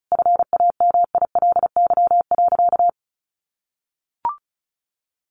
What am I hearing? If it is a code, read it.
Morse code, heard as a keyed tone, about 34 wpm, 700 Hz. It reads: FAMILY.
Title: Sentences from Top 1400 Words 35wpm